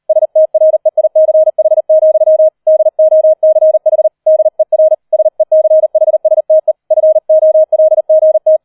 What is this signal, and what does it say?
Morse code, heard as a keyed tone, about 38 wpm, 630 Hz. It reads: STREICH, DOCH DER SECHSTE FOLGT
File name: 40wpm